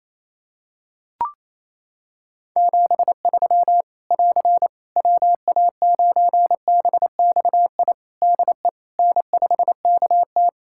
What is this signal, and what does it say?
Morse code, heard as a keyed tone, about 28 wpm, 700 Hz. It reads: 73 <AR> WA9BXI DE N5KT
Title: Practice QSO - Exchange 4 - Ending - 28wpm